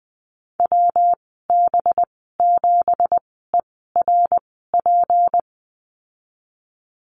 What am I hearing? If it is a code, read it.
Morse code, heard as a keyed tone, about 20 wpm, spaced as for 15 wpm, 700 Hz. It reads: WB7ERP
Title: 186 Lesson 30g - Letter b - Call Signs 20wpm